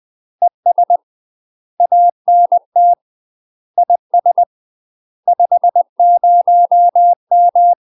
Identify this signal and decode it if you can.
Morse code, heard as a keyed tone, about 20 wpm, 700 Hz. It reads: ES ANT IS 50M
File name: UK MASTER QSOs TRAINING / 20WPM / UK MASTER QSOs 0066